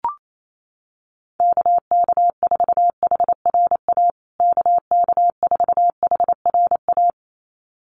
Morse code, heard as a keyed tone, about 28 wpm, 700 Hz. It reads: KK4HRA KK4HRA
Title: USA Call Signs Repeated 2x 28wpm